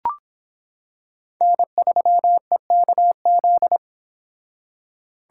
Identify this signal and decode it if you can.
Morse code, heard as a keyed tone, about 25 wpm, 700 Hz. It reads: N3EKZ